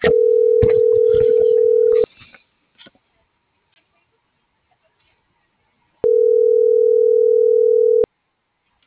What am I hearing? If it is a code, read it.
Ringback tone (North America), 440 + 480 Hz.